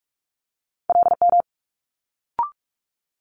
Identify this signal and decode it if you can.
Morse code, heard as a keyed tone, about 46 wpm, 700 Hz. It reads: LM
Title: Multi-letter Phonemes Quick Fire 45wpm